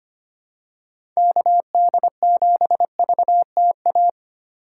Morse code, heard as a keyed tone, about 25 wpm, 700 Hz. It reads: KD7VTA
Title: USA Call Signs - 6 Characters - 2x Character Spacing - 25wpm